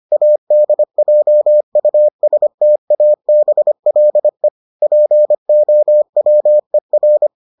Morse code, heard as a keyed tone, about 25 wpm, 600 Hz. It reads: ADJUSTABLE POWER